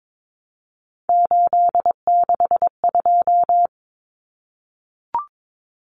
Morse code, heard as a keyed tone, about 22 wpm, 700 Hz. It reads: 862
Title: Numbers - 3-Digit 22wpm